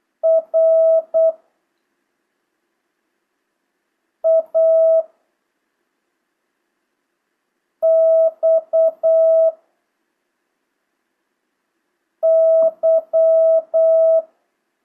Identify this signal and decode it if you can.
Morse code, heard as a keyed tone, about 8 wpm, 650 Hz. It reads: R A X Y
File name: jh8cbh / 2025cw